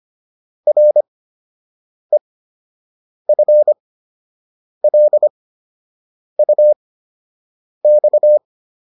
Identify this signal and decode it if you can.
Morse code, heard as a keyed tone, about 25 wpm, spaced as for 7 wpm, 600 Hz. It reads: REFLUX